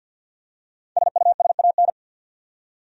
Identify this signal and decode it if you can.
Morse code, heard as a keyed tone, about 49 wpm, 700 Hz. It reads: SUSAN
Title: USA Common Names - Max 5 Letters - 50wpm